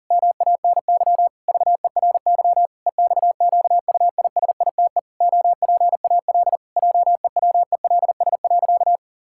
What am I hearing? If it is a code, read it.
Morse code, heard as a keyed tone, about 40 wpm, 700 Hz. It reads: MANY VERY EXQUISITE OPAL JEWELS.
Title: Pangram Challenge v2 - 40wpm